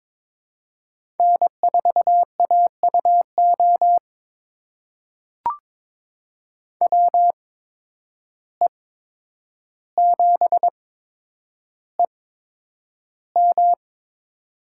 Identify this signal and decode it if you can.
Morse code, heard as a keyed tone, about 22 wpm, spaced as for 9 wpm, 700 Hz. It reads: N4AUO WE7EM
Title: USA Call Signs - 5 Characters - 8x Character Spacing - 22wpm